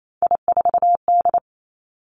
Morse code, heard as a keyed tone, about 28 wpm, 700 Hz. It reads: I4D
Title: Three Challenging Letter-Number Combinations 28wpm